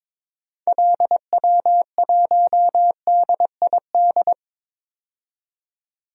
Morse code, heard as a keyed tone, about 21 wpm, 700 Hz. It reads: LW1DID
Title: DXCC-100 Call Signs - 6 Characters - 22wpm